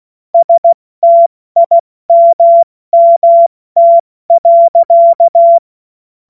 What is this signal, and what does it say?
Morse code, heard as a keyed tone, about 16 wpm, 680 Hz. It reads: STIMMT.
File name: Morsekurs / Witze / Witz 12. April 2022 70 F80